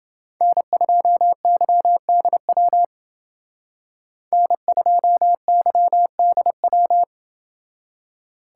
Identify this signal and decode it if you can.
Morse code, heard as a keyed tone, about 29 wpm, 700 Hz. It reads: N2YDW N2YDW